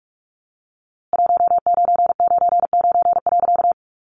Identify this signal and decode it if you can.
Morse code, heard as a keyed tone, about 45 wpm, 700 Hz. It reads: 1999.
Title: A Sound Of Thunder 45wpm